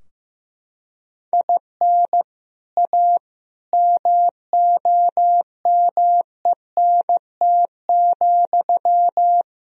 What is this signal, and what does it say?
Morse code, heard as a keyed tone, about 15 wpm, 700 Hz. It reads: IN A MOMENT,
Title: Sentences from Top 1000 Words - Review Part 2 - 15wpm